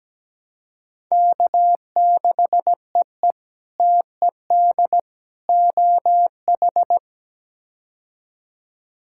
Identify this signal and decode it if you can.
Morse code, heard as a keyed tone, about 17 wpm, 700 Hz. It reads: K6EE TED OH
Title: K1USN SST QSO Practice - 17wpm